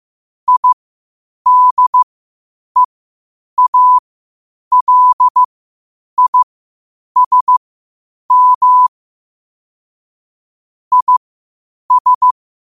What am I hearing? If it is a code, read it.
Morse code, heard as a keyed tone, about 15 wpm, spaced as for 8 wpm, 1000 Hz. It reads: IDEALISM IS
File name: qotdfiles / 8wpm